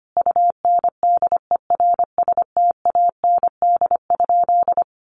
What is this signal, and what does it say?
Morse code, heard as a keyed tone, about 25 wpm, 700 Hz. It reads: UNDERSTAND?
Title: Sentences from Top 1500 Words 25wpm